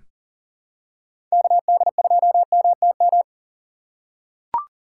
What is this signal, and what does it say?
Morse code, heard as a keyed tone, about 40 wpm, 700 Hz. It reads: KD2MTM
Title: USA Call Signs - 6 Characters - 6x Character Spacing - 40wpm